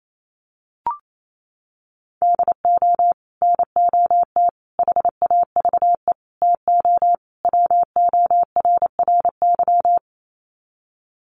Morse code, heard as a keyed tone, about 28 wpm, 700 Hz. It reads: DO NOT HAVE TO WORRY
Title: Sets of 5 Words 28wpm